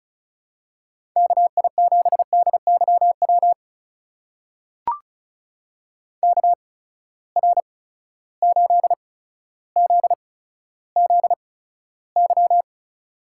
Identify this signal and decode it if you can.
Morse code, heard as a keyed tone, about 36 wpm, spaced as for 15 wpm, 700 Hz. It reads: KI7DYW KR8ZZY
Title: USA Call Signs - 8x Character Spacing - 35wpm